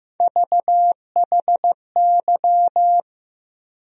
Morse code, heard as a keyed tone, about 15 wpm, 700 Hz. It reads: VHY
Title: USA Call Sign Postfixes - 3 Letters 15wpm